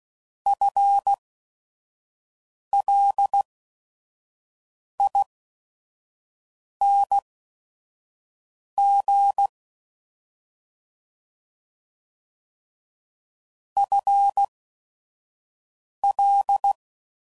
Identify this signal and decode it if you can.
Morse code, heard as a keyed tone, about 16 wpm, spaced as for 5 wpm, 800 Hz. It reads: FLING FL
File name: Prefixes / FLs & THs / FL Prefixes 1